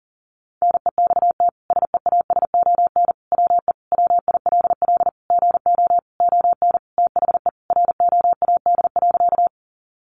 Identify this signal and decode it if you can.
Morse code, heard as a keyed tone, about 37 wpm, 700 Hz. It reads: NEXT SEASON WE WILL GO ON THE ROAD.
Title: Sentences from Top 1400 Words 40wpm